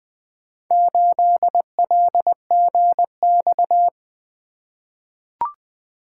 Morse code, heard as a keyed tone, about 20 wpm, 700 Hz. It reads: 8LGX
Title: Four Letter-Number Combinations - 20wpm